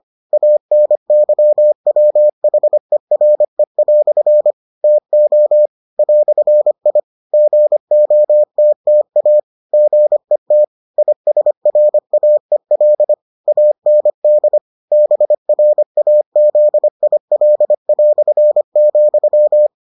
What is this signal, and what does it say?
Morse code, heard as a keyed tone, about 25 wpm, 600 Hz. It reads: ANYWHERE" TO "I GOTTA GET ISRAEL AND BRAZIL",